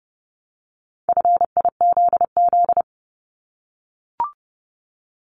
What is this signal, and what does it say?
Morse code, heard as a keyed tone, about 30 wpm, 700 Hz. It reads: FIZZ
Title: Multi-letter Phonemes Repeated 3x 30wpm